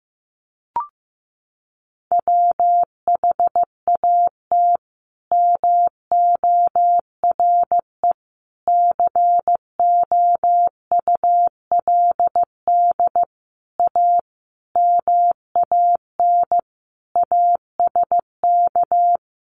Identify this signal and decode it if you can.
Morse code, heard as a keyed tone, about 15 wpm, 700 Hz. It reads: WHAT MORE COULD A MAN ASK
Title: Sentences from Top 200 Words 15wpm